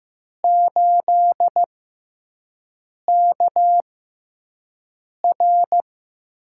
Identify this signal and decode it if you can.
Morse code, heard as a keyed tone, about 15 wpm, spaced as for 5 wpm, 700 Hz. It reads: 8KR
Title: USA Call Signs - 4 Characters - 6x Character Spacing - 15wpm